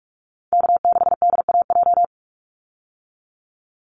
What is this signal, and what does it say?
Morse code, heard as a keyed tone, about 44 wpm, 700 Hz. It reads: K6DAJ
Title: Call Signs Repeated 2x - 45wpm